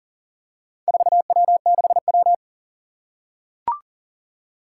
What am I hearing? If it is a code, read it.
Morse code, heard as a keyed tone, about 37 wpm, 700 Hz. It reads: 4W6W
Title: Four Challenging Letter-Number Combinations 40wpm